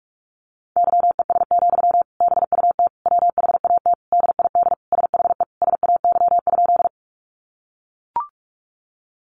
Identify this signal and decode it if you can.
Morse code, heard as a keyed tone, about 44 wpm, 700 Hz. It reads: YES, BUT WHAT DID SHE SAY?
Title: Sentences from Top 500 Words 45wpm